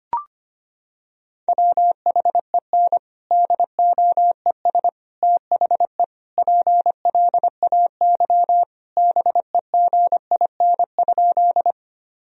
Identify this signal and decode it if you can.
Morse code, heard as a keyed tone, about 25 wpm, 700 Hz. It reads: WHEN DOES THE PLAY BEGIN?
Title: Sentences from Top 1000 Words - Review Part 1 - 25wpm